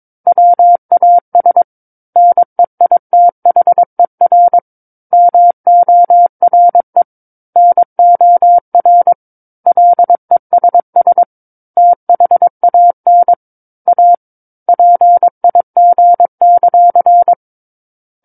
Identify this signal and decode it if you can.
Morse code, heard as a keyed tone, about 22 wpm, 700 Hz. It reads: WAS NEITHER MORE NOR LESS THAN A PIG;
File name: 06-Chapter-06 22 WPM